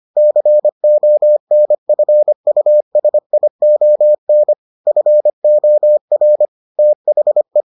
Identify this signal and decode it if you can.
Morse code, heard as a keyed tone, about 25 wpm, 600 Hz. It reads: CONFUSION FOR THE